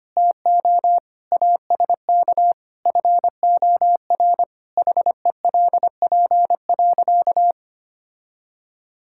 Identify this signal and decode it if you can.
Morse code, heard as a keyed tone, about 25 wpm, 700 Hz. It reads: TO ASK FOR HELP.
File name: practice / 25wpm / Sentences with 2-Word Phrases - 25wpm